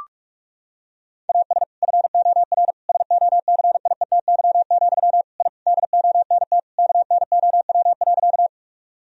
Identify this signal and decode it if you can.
Morse code, heard as a keyed tone, about 42 wpm, 700 Hz. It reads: AS FOR SOCIETY, I DONT KNOW.
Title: Sentences from Top 1000 Words - Review Part 4 - 45wpm